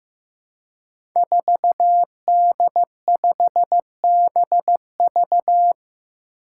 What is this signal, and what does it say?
Morse code, heard as a keyed tone, about 15 wpm, 700 Hz. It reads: 4D5BV
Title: Five Challenging Letter-Number Combinations 15wpm